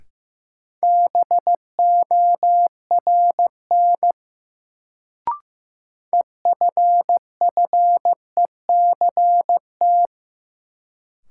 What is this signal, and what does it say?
Morse code, heard as a keyed tone, about 15 wpm, 700 Hz. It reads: BORN EFFECT
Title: Sentences from Top 1000 Words - New Word Drill - 15wpm